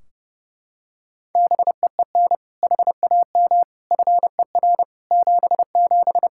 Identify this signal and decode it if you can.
Morse code, heard as a keyed tone, about 30 wpm, 700 Hz. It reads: BEEN HAM FER 77
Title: Practice QSO - Exchange 3 - More Info - 30wpm